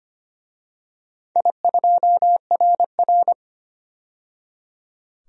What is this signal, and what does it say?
Morse code, heard as a keyed tone, about 25 wpm, 700 Hz. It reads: I2RR
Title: DXCC-100 Call Signs - 4 Characters - 25wpm